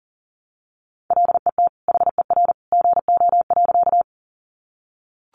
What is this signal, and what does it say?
Morse code, heard as a keyed tone, about 40 wpm, 700 Hz. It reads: LET HER GO.
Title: Sentences from Top 300 Words 40wpm